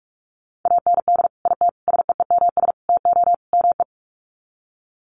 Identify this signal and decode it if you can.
Morse code, heard as a keyed tone, about 44 wpm, 700 Hz. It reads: AND IT SEEMS TO ME
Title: Sets of 5 Words Encore2 45wpm